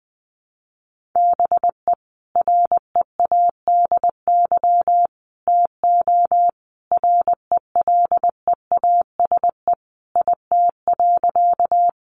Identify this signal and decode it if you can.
Morse code, heard as a keyed tone, about 20 wpm, 700 Hz. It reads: BE READY TO RELEASE IT.
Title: Sentences from Top 1100 Words 20wpm